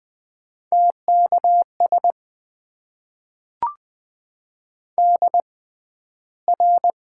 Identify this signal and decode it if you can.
Morse code, heard as a keyed tone, about 20 wpm, spaced as for 12 wpm, 700 Hz. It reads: TKS DR
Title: Top 90 QSO Elements - 6x Character Spacing - 20wpm